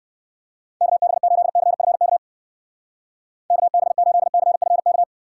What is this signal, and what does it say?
Morse code, heard as a keyed tone, about 59 wpm, 700 Hz. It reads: KB8CFK KB8CFK